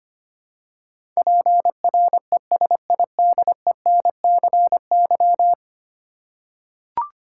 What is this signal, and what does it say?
Morse code, heard as a keyed tone, about 25 wpm, 700 Hz. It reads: PRESIDENCY
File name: Ten Letter Words - 25wpm